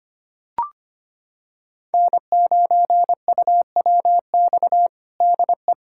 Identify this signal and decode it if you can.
Morse code, heard as a keyed tone, about 25 wpm, 700 Hz. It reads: N9UWX DE
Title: USA Callsign DE Callsign - 5 Characters - 25wpm